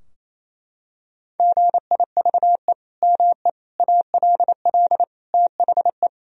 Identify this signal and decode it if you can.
Morse code, heard as a keyed tone, about 28 wpm, 700 Hz. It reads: GIVE ME ALL THE
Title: Sentences from Top 1000 Words - Review Part 3 - 28wpm